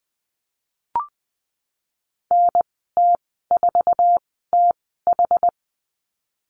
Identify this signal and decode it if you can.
Morse code, heard as a keyed tone, about 20 wpm, spaced as for 15 wpm, 700 Hz. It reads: NT4TH
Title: 154 Lesson 25g - Letter p - Call Signs 20wpm